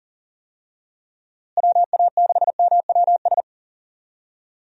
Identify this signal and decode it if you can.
Morse code, heard as a keyed tone, about 40 wpm, 700 Hz. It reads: WA6MWS
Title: USA Call Signs - 6 Characters - Repeated 3x - 40wpm